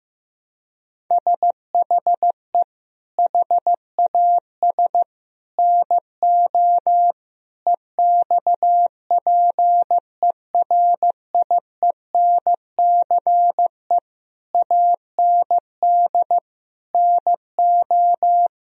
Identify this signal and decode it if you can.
Morse code, heard as a keyed tone, about 15 wpm, 700 Hz. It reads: SHE HAS NO EXPERIENCE AND NO
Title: Sentences from Top 1000 Words - Review Part 4 - 15wpm